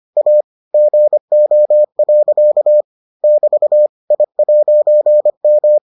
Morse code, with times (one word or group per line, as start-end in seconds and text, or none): 0.2-0.4 A
0.7-2.8 GO.
3.2-3.9 <BT>
4.1-5.8 I'M